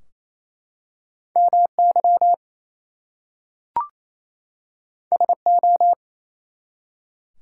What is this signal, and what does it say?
Morse code, heard as a keyed tone, about 27 wpm, 700 Hz. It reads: MY SO
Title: Two Letter Words - 28wpm